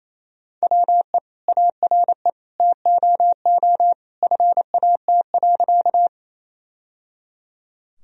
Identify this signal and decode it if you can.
Morse code, heard as a keyed tone, about 28 wpm, 700 Hz. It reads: WE ARE TOO FAT.
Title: Sentences from Top 1000 Words - Review Part 4 - 28wpm